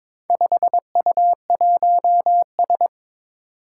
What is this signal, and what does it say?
Morse code, heard as a keyed tone, about 22 wpm, 700 Hz. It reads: 5U1S
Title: Four Challenging Letter-Number Combinations 22wpm